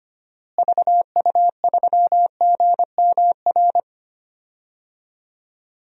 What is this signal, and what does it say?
Morse code, heard as a keyed tone, about 25 wpm, 700 Hz. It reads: VU3GMR
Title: DXCC-100 Call Signs - 6 Characters - Repeated 2x - 25wpm